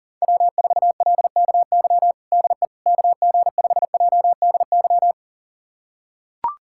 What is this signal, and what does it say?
Morse code, heard as a keyed tone, about 37 wpm, 700 Hz. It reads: W4LKY DE KG5JDY
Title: USA Callsign DE Callsign 40wpm